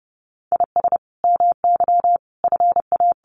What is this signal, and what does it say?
Morse code, heard as a keyed tone, about 30 wpm, 700 Hz. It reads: IS MY FA
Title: Sentences from Top 500 Words 30wpm